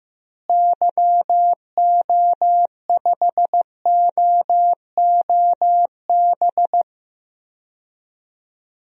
Morse code, heard as a keyed tone, about 15 wpm, 700 Hz. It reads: YO5OOB
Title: DXCC-100 Call Signs - 6 Characters - Repeated 3x - 15wpm